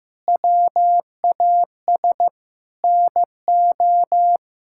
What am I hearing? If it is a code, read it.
Morse code, heard as a keyed tone, about 15 wpm, 700 Hz. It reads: WAS NO
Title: Sentences from Top 1000 Words - Review Part 6 - 15wpm